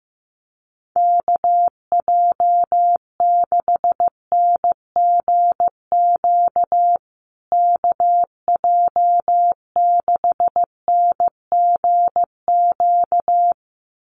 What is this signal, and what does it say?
Morse code, heard as a keyed tone, about 15 wpm, 700 Hz. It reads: KJ6NGQ KJ6NGQ
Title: USA Call Signs Repeated 2x 15wpm